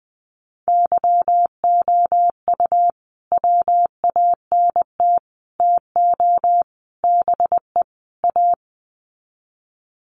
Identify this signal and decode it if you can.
Morse code, heard as a keyed tone, about 20 wpm, 700 Hz. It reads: YOU WANT TO BE A
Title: Sets of 5 Words Encore2 20wpm